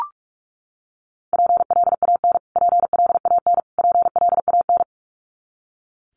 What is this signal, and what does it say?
Morse code, heard as a keyed tone, about 44 wpm, 700 Hz. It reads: PLAN PLAN PLAN
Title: Top 500 Words Repeated 45wpm